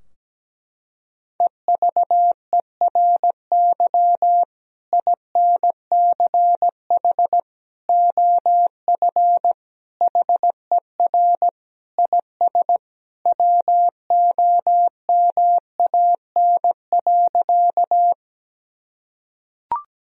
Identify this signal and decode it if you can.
Morse code, heard as a keyed tone, about 17 wpm, 700 Hz. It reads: EVERY INCH OF HER IS WOMAN.
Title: Sentences from Top 1000 Words - Review Part 3 - 17wpm